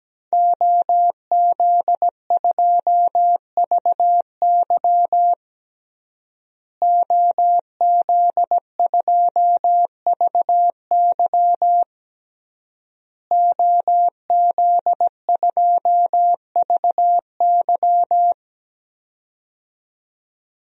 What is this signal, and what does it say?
Morse code, heard as a keyed tone, about 17 wpm, 700 Hz. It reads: OZ2VY OZ2VY OZ2VY
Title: DXCC-100 Call Signs - 5 Characters - Repeated 3x - 17wpm